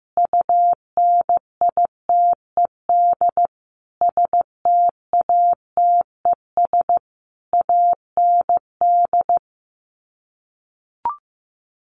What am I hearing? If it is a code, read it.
Morse code, heard as a keyed tone, about 15 wpm, 700 Hz. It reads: UNITED STATES AND
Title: Sets of 5 Words 15wpm